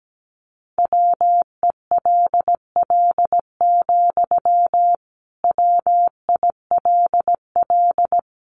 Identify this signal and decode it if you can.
Morse code, heard as a keyed tone, about 17 wpm, 700 Hz. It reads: WELL, WILL
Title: Sentences from Top 1300 Words 17wpm